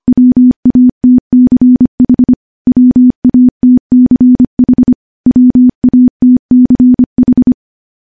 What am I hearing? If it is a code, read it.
Morse code, heard as a keyed tone, about 25 wpm, 250 Hz. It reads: WATCH WATCH WATCH
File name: morse / morse-training-25WPM